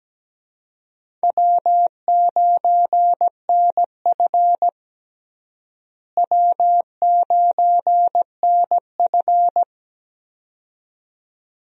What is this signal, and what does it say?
Morse code, heard as a keyed tone, about 17 wpm, 700 Hz. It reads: W9NF W9NF
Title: USA Call Signs - 4 Characters - Repeated 2x - 17wpm